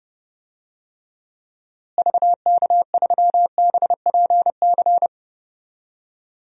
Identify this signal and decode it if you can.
Morse code, heard as a keyed tone, about 30 wpm, 700 Hz. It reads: VK3BPC
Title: DXCC-100 Call Signs - 6 Characters - Repeated 3x - 30wpm